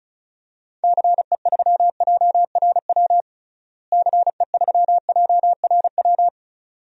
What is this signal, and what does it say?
Morse code, heard as a keyed tone, about 36 wpm, 700 Hz. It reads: CE3JRW CE3JRW
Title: DXCC-100 Call Signs - 6 Characters - Repeated 2x - 35wpm